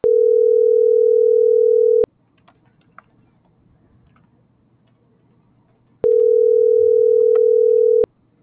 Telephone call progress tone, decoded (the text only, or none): ringback tone